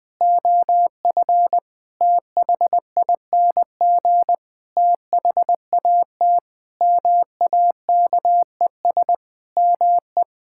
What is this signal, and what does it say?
Morse code, heard as a keyed tone, about 20 wpm, 700 Hz. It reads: OF THING THAT MAKES ME
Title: Sentences with 7-Word Phrases - 20wpm